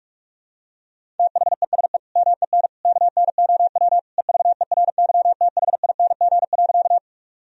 Morse code, heard as a keyed tone, about 41 wpm, 700 Hz. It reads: THESE MEN KNOW EVERYTHING.
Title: Sentences from Top 1000 Words - Review Part 1 - 45wpm